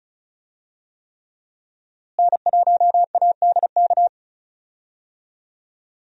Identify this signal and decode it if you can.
Morse code, heard as a keyed tone, about 35 wpm, 700 Hz. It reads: N1ADK
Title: USA Call Signs - 5 Characters - Repeated 3x - 35wpm